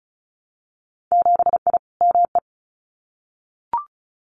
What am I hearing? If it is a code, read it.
Morse code, heard as a keyed tone, about 34 wpm, 700 Hz. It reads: 7I ME
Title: Winter Field Day Exchanges Encore 35wpm